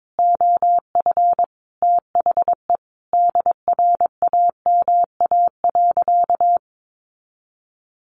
Morse code, heard as a keyed tone, about 22 wpm, 700 Hz. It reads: OF THE DRAMA.
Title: Sentences from Top 1500 Words 22wpm